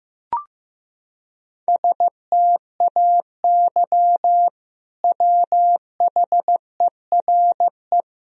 Morse code, heard as a keyed tone, about 15 wpm, 700 Hz. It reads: STAY WHERE
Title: Sentences from Top 500 Words and Abbreviations 15wpm